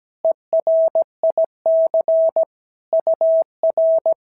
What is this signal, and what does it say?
Morse code, heard as a keyed tone, about 17 wpm, 690 Hz. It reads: ERIC UR
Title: SOTA QSO Practice 17wpm